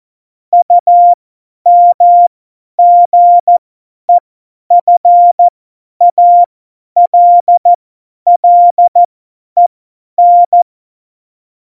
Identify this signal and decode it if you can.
Morse code, heard as a keyed tone, about 14 wpm, spaced as for 10 wpm, 700 Hz. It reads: UMGEFALLEN
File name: Morsekurs / Witze / Witz 07. Januar 2020, 50 F70